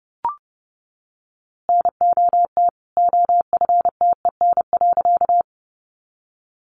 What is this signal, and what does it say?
Morse code, heard as a keyed tone, about 30 wpm, 700 Hz. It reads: NOT OFTEN.
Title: A Sound Of Thunder 30wpm